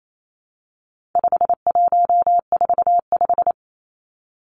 Morse code, heard as a keyed tone, about 28 wpm, 700 Hz. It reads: 5145
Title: Numbers - 4-Digit 28wpm